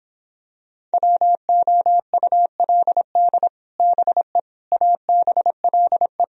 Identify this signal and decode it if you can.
Morse code, heard as a keyed tone, about 26 wpm, 700 Hz. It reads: WOULD BE ABLE